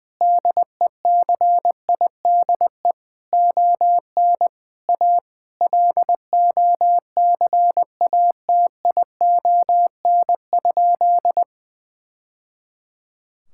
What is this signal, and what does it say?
Morse code, heard as a keyed tone, about 20 wpm, 700 Hz. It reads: DECIDE ON A LOCATION?
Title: Sentences from Top 1000 Words - Review Part 6 - 20wpm